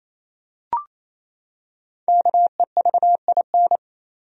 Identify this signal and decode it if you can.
Morse code, heard as a keyed tone, about 28 wpm, 700 Hz. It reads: KEVIN
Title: USA Common Names - 28wpm